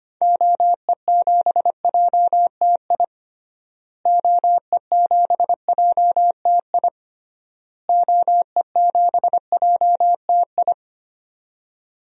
Morse code, heard as a keyed tone, about 25 wpm, 700 Hz. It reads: OE7JTI OE7JTI OE7JTI
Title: DXCC-100 Call Signs - 6 Characters - Repeated 3x - 25wpm